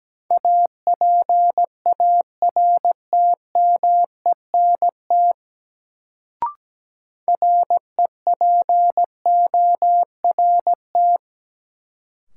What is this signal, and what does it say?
Morse code, heard as a keyed tone, about 17 wpm, 700 Hz. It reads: APARTMENT REPORT
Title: Sentences from Top 900 Words - New Word Drill - 17wpm